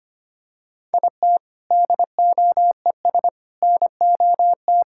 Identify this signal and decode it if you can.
Morse code, heard as a keyed tone, about 25 wpm, 700 Hz. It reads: IT DOES NOT
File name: Sentences with 3-Word Phrases - 25wpm